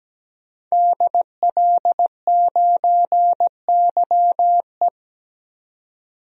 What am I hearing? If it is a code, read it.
Morse code, heard as a keyed tone, about 17 wpm, 700 Hz. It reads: DL9YE
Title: DXCC-100 Call Signs - 5 Characters - 17wpm